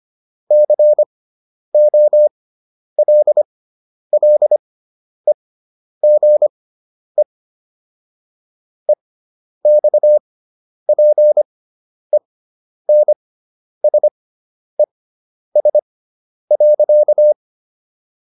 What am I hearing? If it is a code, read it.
Morse code, heard as a keyed tone, about 25 wpm, spaced as for 10 wpm, 600 Hz. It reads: COLLEGE EXPENSES.